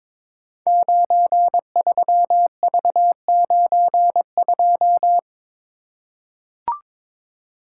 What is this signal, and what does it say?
Morse code, heard as a keyed tone, about 22 wpm, 700 Hz. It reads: 93V92
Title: Code Groups - 5 Letter:Numbers - 22wpm